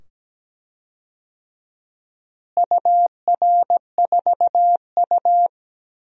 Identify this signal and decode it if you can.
Morse code, heard as a keyed tone, about 17 wpm, 700 Hz. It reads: UR4U
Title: DXCC-100 Call Signs - 4 Characters - Repeated 2x - 17wpm